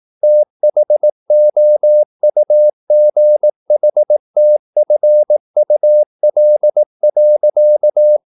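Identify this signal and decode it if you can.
Morse code, heard as a keyed tone, about 18 wpm, 600 Hz. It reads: THOUGHTFUL.
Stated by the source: Morse practice at 18 wpm, English text, from CW med Gnister.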